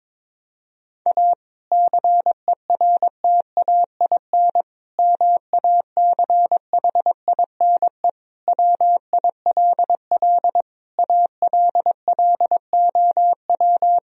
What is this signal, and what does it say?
Morse code, heard as a keyed tone, about 22 wpm, 700 Hz. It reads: A CERTAIN MACHINE WILL ALLOW
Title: Sentences from Top 1000 Words - Review Part 2 - 22wpm